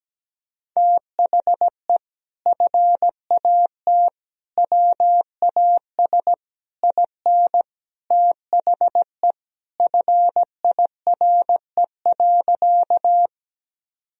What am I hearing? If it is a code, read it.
Morse code, heard as a keyed tone, about 17 wpm, 700 Hz. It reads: THE FAT WAS IN THE FIRE.
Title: Sentences from Top 1000 Words - Review Part 4 - 17wpm